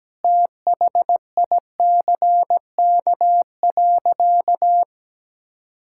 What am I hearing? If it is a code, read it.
Morse code, heard as a keyed tone, about 17 wpm, 700 Hz. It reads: THICK.
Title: Sentences from Top 1600 Words 17wpm